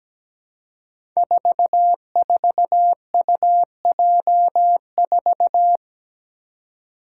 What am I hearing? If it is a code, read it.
Morse code, heard as a keyed tone, about 17 wpm, 700 Hz. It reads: 44UJ4
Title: Five Challenging Letter-Number Combinations 17wpm